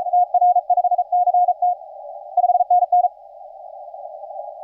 Morse code, heard as a keyed tone, about 36 wpm, 720 Hz. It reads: TR5CT 5NN
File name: posts / ri1f-eu-190-20cw